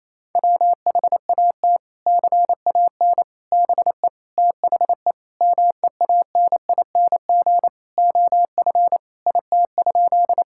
Morse code, heard as a keyed tone, about 28 wpm, 700 Hz. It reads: WHAT CAN BE THE MEANING OF IT?
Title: Sentences from Top 1000 Words - Review Part 6 - 28wpm